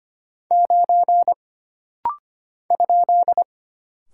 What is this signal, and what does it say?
Morse code, heard as a keyed tone, about 25 wpm, 700 Hz. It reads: 9 ?
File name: Single Character (Letter:Number:Punctuation:Prosign) - Rapid-Fire - 25wpm